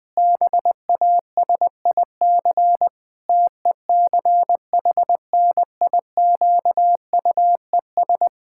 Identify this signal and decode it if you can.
Morse code, heard as a keyed tone, about 21 wpm, 700 Hz. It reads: BASIC TECHNIQUES
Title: Pangram Challenge v2 - 20wpm